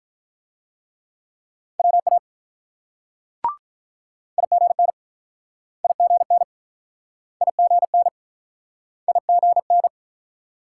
Morse code, heard as a keyed tone, about 48 wpm, 700 Hz. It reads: WA IGN IGN IGN IGN